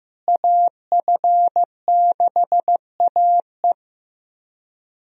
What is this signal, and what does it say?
Morse code, heard as a keyed tone, about 15 wpm, 700 Hz. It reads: AF6AE
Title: USA Call Signs - 5 Characters - Repeated 3x - 15wpm